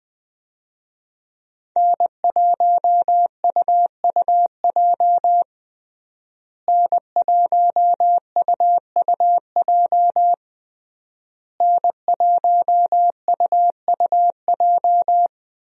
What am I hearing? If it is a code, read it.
Morse code, heard as a keyed tone, about 20 wpm, 700 Hz. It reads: N1UUJ N1UUJ N1UUJ
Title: USA Call Signs - 5 Characters - Repeated 3x - 20wpm